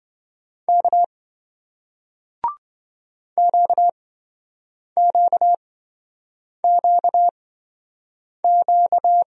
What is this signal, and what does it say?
Morse code, heard as a keyed tone, about 27 wpm, spaced as for 7 wpm, 700 Hz. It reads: K QQQQ